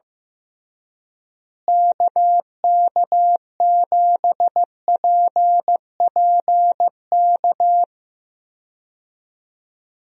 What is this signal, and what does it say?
Morse code, heard as a keyed tone, about 15 wpm, 700 Hz. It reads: KK7PPK
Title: USA Call Signs - 6 Characters - Repeated 2x - 15wpm